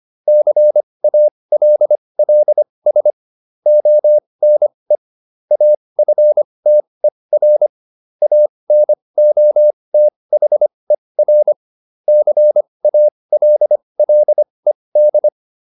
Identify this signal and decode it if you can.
Morse code, heard as a keyed tone, about 25 wpm, spaced as for 20 wpm, 600 Hz. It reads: CALLS ONE AFTER ANOTHER CALLED